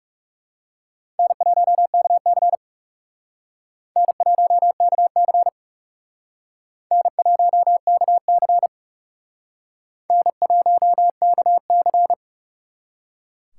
Morse code, heard as a keyed tone, about 40 wpm, 700 Hz. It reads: N1KC N1KC N1KC N1KC